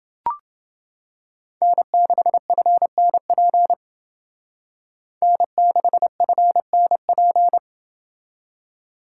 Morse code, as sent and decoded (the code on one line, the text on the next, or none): -. -.... ..-. -. .--. / -. -.... ..-. -. .--.
N6FNP N6FNP